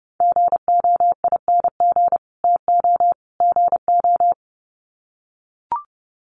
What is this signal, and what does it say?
Morse code, heard as a keyed tone, about 30 wpm, 700 Hz. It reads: GOING TO GO
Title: Sets of 3 Words 30wpm